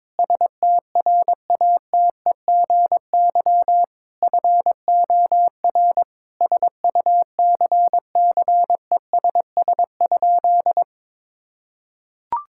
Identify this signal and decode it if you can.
Morse code, heard as a keyed tone, about 22 wpm, 700 Hz. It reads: STRATEGY FOR SUCCESS?
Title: Sentences from Top 1000 Words - Review Part 5 - 22wpm